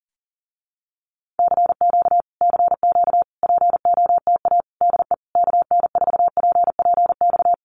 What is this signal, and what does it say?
Morse code, heard as a keyed tone, about 40 wpm, 700 Hz. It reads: CQ CQ POTA DE KN4PPX
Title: Common QSO Phrases Encore 40wpm